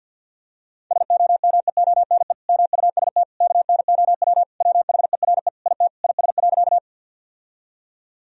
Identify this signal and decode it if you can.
Morse code, heard as a keyed tone, about 50 wpm, 700 Hz. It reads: SOMEONE MUST KNOW WHERE IT IS.
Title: Sentences from Top 1000 Words - Review Part 4 - 50wpm